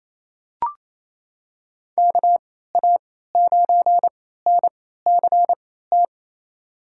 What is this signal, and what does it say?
Morse code, heard as a keyed tone, about 28 wpm, spaced as for 16 wpm, 700 Hz. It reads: KA9NCT
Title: USA Call Signs - 6 Characters - 3x Character Spacing - 28wpm